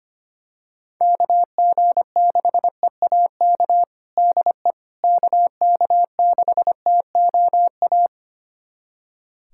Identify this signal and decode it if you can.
Morse code, heard as a keyed tone, about 25 wpm, 700 Hz. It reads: KG6EAK DE KK6TOA
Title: USA Callsign DE Callsign - 6 Characters - 25wpm